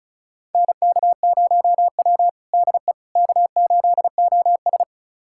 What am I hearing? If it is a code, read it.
Morse code, heard as a keyed tone, about 36 wpm, 700 Hz. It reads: NK0W DE K8OS
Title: USA Callsign DE Callsign - 4 Characters - 35wpm